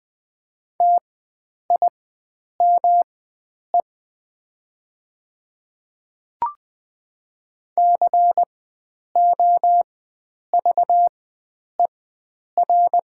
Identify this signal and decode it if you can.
Morse code, heard as a keyed tone, about 20 wpm, spaced as for 9 wpm, 700 Hz. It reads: TIME COVER
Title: 4x Character Spacing - Top 500 Words - 20wpm